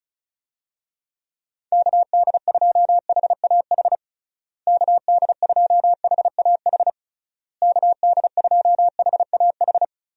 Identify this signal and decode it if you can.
Morse code, heard as a keyed tone, about 34 wpm, 700 Hz. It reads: KD2HAH KD2HAH KD2HAH
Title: USA Call Signs - 6 Characters - Repeated 3x - 35wpm